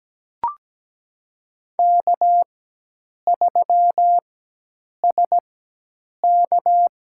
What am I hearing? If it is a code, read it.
Morse code, heard as a keyed tone, about 17 wpm, spaced as for 8 wpm, 700 Hz. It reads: K3SK
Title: USA Call Signs - 4 Characters - 4x Character Spacing - 17wpm